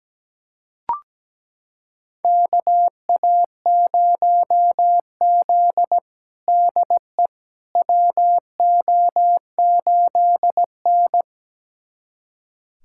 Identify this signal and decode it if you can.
Morse code, heard as a keyed tone, about 17 wpm, 700 Hz. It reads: KA0Z DE WO8N
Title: USA Callsign DE Callsign - 4 Characters - 17wpm